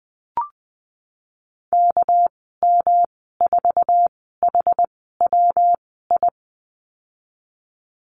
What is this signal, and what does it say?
Morse code, heard as a keyed tone, about 20 wpm, spaced as for 15 wpm, 700 Hz. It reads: KM4HWI
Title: 213 Lesson 34g - Number 8 - Call Signs 20wpm